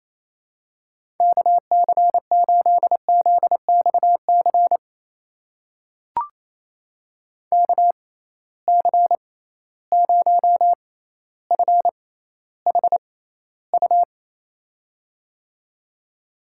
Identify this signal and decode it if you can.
Morse code, heard as a keyed tone, about 28 wpm, spaced as for 14 wpm, 700 Hz. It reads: KC8ZXC KC0FHU
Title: USA Call Signs - 6 Characters - 6x Character Spacing - 28wpm